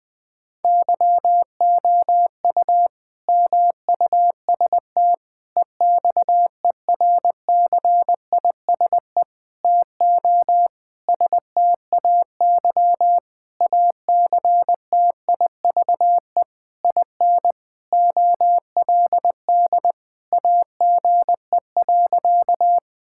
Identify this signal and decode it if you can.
Morse code, heard as a keyed tone, about 20 wpm, 700 Hz. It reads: YOU MUST EXERCISE TO STAY ACTIVE IN OLD AGE.
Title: Sentences from Top 1000 Words - Review Part 6 - 20wpm